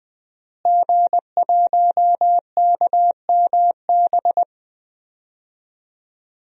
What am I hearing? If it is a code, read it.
Morse code, heard as a keyed tone, about 21 wpm, 700 Hz. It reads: G1KMB
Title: DXCC-100 Call Signs - 5 Characters - Repeated 2x - 20wpm